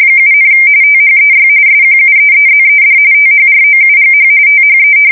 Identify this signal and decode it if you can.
RTTY, about 45.45 baud. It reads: 7, THEY SQUEEZED THROUGH THE
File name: fsk170 / battleshipnj